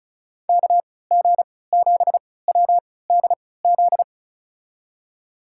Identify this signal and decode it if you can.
Morse code, heard as a keyed tone, about 36 wpm, spaced as for 20 wpm, 700 Hz. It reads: KG7WDZ
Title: USA Call Signs - 6 Characters - 3x Character Spacing - 35wpm